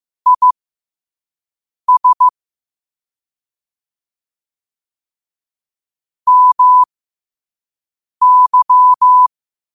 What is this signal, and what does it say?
Morse code, heard as a keyed tone, about 15 wpm, spaced as for 5 wpm, 1000 Hz. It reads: IS MY
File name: qotdfiles / 5wpm